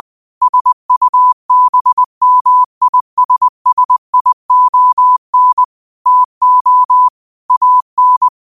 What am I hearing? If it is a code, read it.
Morse code, heard as a keyed tone, about 20 wpm, 1000 Hz. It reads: SUBMISSION TO AN
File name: qotdfiles / 20wpm